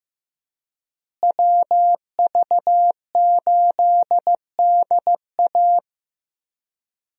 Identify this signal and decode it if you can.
Morse code, heard as a keyed tone, about 15 wpm, 700 Hz. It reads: WV8DA
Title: USA Call Signs - 5 Characters - Repeated 3x - 15wpm